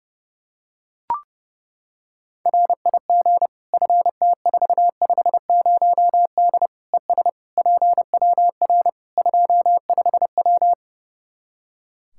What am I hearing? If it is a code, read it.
Morse code, heard as a keyed tone, about 30 wpm, 700 Hz. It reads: RIG FT450D ES PWR 25W
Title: Practice QSO - Exchange 2 - Info - 30wpm